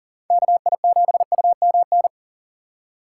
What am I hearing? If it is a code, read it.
Morse code, heard as a keyed tone, about 55 wpm, spaced as for 47 wpm, 700 Hz. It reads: KI7UMN
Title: Speed Racing - USA 2x3 Call Signs 40wpm